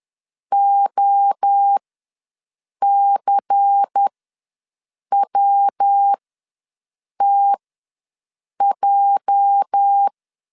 Morse code, heard as a keyed tone, about 10 wpm, spaced as for 6 wpm, 800 Hz. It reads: OCWTJ
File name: LEZIONI / 9 28 ws 10wpm 4124